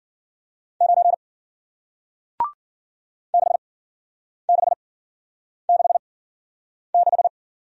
Morse code, heard as a keyed tone, about 53 wpm, spaced as for 9 wpm, 700 Hz. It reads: 9 6666